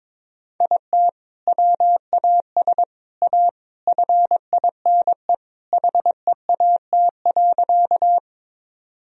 Morse code, heard as a keyed tone, about 22 wpm, 700 Hz. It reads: IT WAS A FINE HEAT.
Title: Sentences from Top 1000 Words - Review Part 3 - 22wpm